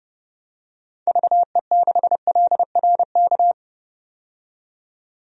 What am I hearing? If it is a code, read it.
Morse code, heard as a keyed tone, about 32 wpm, 700 Hz. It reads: VE6LRK
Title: DXCC-100 Call Signs - 6 Characters - 30wpm